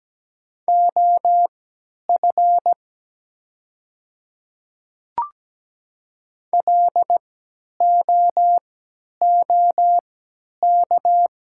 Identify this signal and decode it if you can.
Morse code, heard as a keyed tone, about 17 wpm, spaced as for 10 wpm, 700 Hz. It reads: OF LOOK
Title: 3x Character Spacing - Top 200 Words - 17wpm